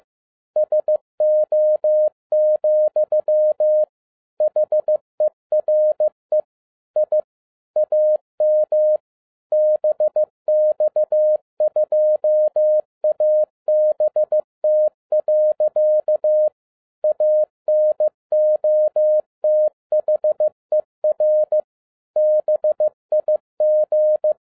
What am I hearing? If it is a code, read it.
Morse code, heard as a keyed tone, about 15 wpm, 610 Hz. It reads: SO, HERE I AM BX2ABT. ANOTHER BIG